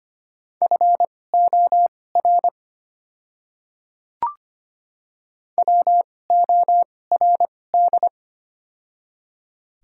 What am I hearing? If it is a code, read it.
Morse code, heard as a keyed tone, about 25 wpm, spaced as for 18 wpm, 700 Hz. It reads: FOR WORD
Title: 2x Character Spacing - Top 100 Words - 25wpm